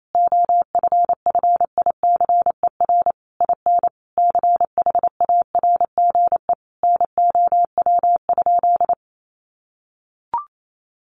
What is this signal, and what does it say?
Morse code, heard as a keyed tone, about 28 wpm, 700 Hz. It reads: OFFICER IN CHARGE NOW?
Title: Sentences from Top 1200 Words 28wpm